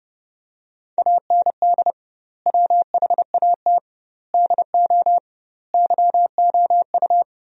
Morse code, heard as a keyed tone, about 29 wpm, 700 Hz. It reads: AND WHAT DO YOU
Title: 2x Word-Spacing - Sentences from Top 100 Words - 30wpm